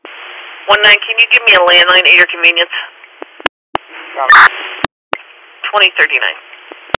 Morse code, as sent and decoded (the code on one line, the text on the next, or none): none